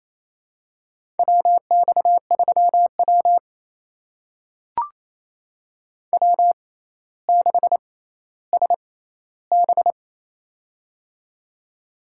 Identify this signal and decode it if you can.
Morse code, heard as a keyed tone, about 28 wpm, spaced as for 14 wpm, 700 Hz. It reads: WX3W W6SB
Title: USA Call Signs - 4 Characters - 6x Character Spacing - 28wpm